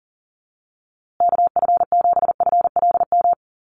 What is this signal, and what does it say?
Morse code, heard as a keyed tone, about 40 wpm, 700 Hz. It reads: KF7FLM
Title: USA Call Signs Repeated 3x 40wpm